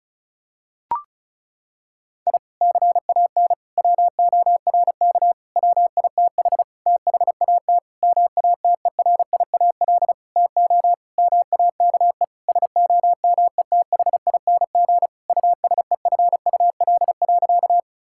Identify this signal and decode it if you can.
Morse code, heard as a keyed tone, about 36 wpm, 690 Hz. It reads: I CAN WORK WITH THAT MATERIAL TO MAKE SOMETHING USEFUL.
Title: Sentences from Top 1000 Words - Review Part 5 - 35wpm